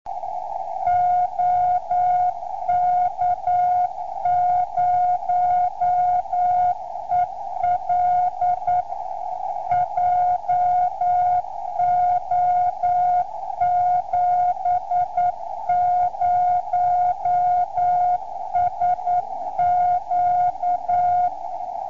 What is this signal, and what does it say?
Morse code, heard as a keyed tone, about 9 wpm, 710 Hz. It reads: OK0EL JO70SQ